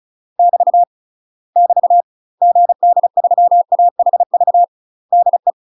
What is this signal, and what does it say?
Morse code, heard as a keyed tone, about 34 wpm, 700 Hz. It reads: <BT> <BT> GD3AHV DE
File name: UK MASTER QSOs TRAINING/35WPM